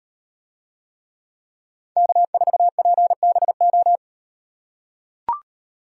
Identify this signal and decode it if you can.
Morse code, heard as a keyed tone, about 38 wpm, 700 Hz. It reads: K4PBO